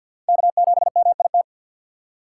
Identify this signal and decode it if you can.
Morse code, heard as a keyed tone, about 48 wpm, 700 Hz. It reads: K7MIT
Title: USA Call Signs - 2x Character Spacing - 50wpm